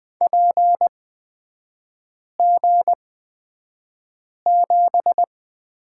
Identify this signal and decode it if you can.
Morse code, heard as a keyed tone, about 20 wpm, spaced as for 5 wpm, 700 Hz. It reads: PG7
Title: 224 Lesson 36e - BT - Three Characters 20wpm